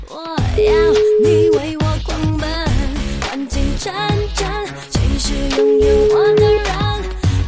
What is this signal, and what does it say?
Ringback tone (Europe, 425 Hz), with noise.